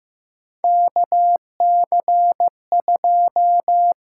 Morse code, heard as a keyed tone, about 15 wpm, 700 Hz. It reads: KC2
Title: USA Call Signs - 6 Characters - 6x Character Spacing - 15wpm